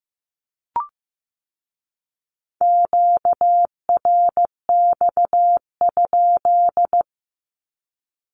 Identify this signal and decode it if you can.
Morse code, heard as a keyed tone, about 15 wpm, 700 Hz. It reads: QRX?
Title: Q-Codes with Spoken Meaning - 15wpm